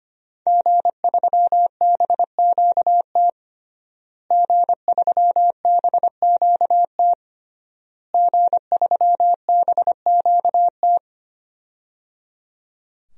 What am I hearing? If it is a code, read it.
Morse code, heard as a keyed tone, about 25 wpm, 700 Hz. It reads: G3BQT G3BQT G3BQT
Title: DXCC-100 Call Signs - 5 Characters - Repeated 3x - 25wpm